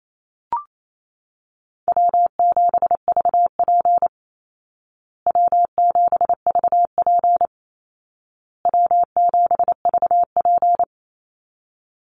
Morse code, heard as a keyed tone, about 28 wpm, 700 Hz. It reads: W7VP W7VP W7VP
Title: Call Signs Repeated 2x - 28wpm